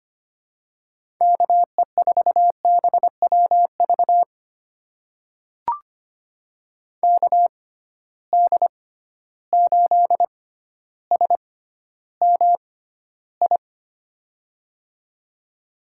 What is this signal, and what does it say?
Morse code, heard as a keyed tone, about 25 wpm, spaced as for 13 wpm, 700 Hz. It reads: KE4BWV KD8SMI
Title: USA Call Signs - 6 Characters - 6x Character Spacing - 25wpm